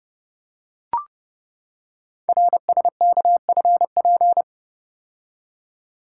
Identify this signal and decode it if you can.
Morse code, heard as a keyed tone, about 29 wpm, 690 Hz. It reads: RSKFP
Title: Code Groups - 5 Letters - 30wpm